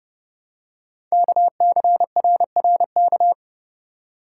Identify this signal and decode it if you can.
Morse code, heard as a keyed tone, about 30 wpm, 700 Hz. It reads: KCRRK
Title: Code Groups - 5 Letter:Numbers - 30wpm